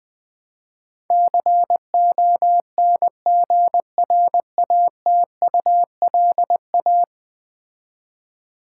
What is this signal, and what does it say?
Morse code, heard as a keyed tone, about 20 wpm, 700 Hz. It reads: CONGRATULA
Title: Build A Word - 5 Letters At A Time - 12-Letter Words - 20wpm